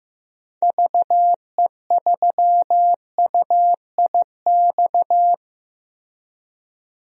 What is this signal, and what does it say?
Morse code, heard as a keyed tone, about 15 wpm, 700 Hz. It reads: VE3UIX
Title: DXCC-100 Call Signs - 6 Characters - 15wpm